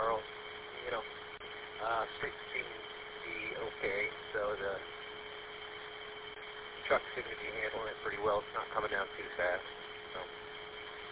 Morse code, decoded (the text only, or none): none